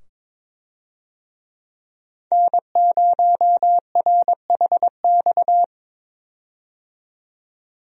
Morse code, heard as a keyed tone, about 22 wpm, 700 Hz. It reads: N0RHX
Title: USA Call Signs - 5 Characters - 22wpm